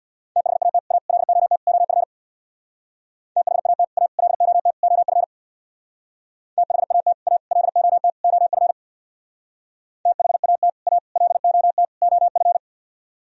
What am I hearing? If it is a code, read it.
Morse code, heard as a keyed tone, about 63 wpm, 700 Hz. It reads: THAT A LOT OF THAT A LOT OF THAT A LOT OF THAT A LOT OF